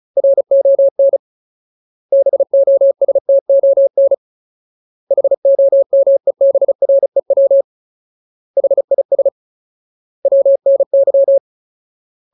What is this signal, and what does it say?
Morse code, heard as a keyed tone, about 34 wpm, 550 Hz. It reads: RON BOSTON HOMEBREW HIS WNY
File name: Ultimate Word List / 35WPM Ultimate